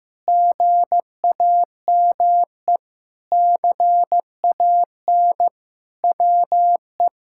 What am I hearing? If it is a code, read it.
Morse code, heard as a keyed tone, about 15 wpm, 700 Hz. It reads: GAME CAN WE
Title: Sentences from Top 1000 Words - Review Part 6 - 15wpm